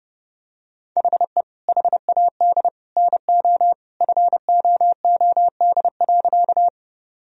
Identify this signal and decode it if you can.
Morse code, heard as a keyed tone, about 32 wpm, 700 Hz. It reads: HE HAD NO FOOD.
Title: Sentences from Top 1000 Words - Review Part 1 - 30wpm